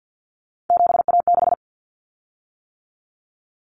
Morse code, heard as a keyed tone, about 50 wpm, 700 Hz. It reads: 7A6
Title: Common Cut Numbers - 3-Digit 50wpm